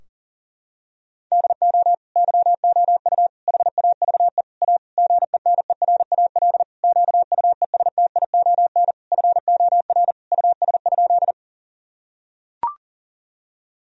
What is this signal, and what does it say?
Morse code, heard as a keyed tone, about 40 wpm, 700 Hz. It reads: DO YOU HAVE A GENERAL QUESTION FOR US?
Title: Sentences from Top 1000 Words - Review Part 3 - 40wpm